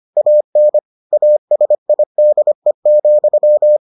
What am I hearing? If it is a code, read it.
Morse code, heard as a keyed tone, about 25 wpm, 600 Hz. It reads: AN ASIDE,